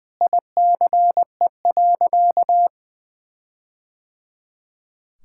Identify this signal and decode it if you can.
Morse code, heard as a keyed tone, about 20 wpm, 700 Hz. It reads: ICE.
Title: Sentences from Top 1600 Words 20wpm